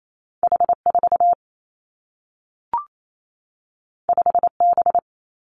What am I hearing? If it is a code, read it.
Morse code, heard as a keyed tone, about 28 wpm, 700 Hz. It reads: H 4 5 B
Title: Two Challenging Letter-Number Combinations 28wpm